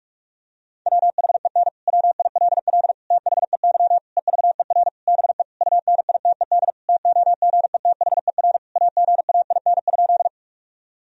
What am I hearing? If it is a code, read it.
Morse code, heard as a keyed tone, about 46 wpm, 700 Hz. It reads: WHEN WILL THEY EVER BE UNITED TOGETHER AGAIN?
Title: Sentences from Top 1000 Words - Review Part 4 - 45wpm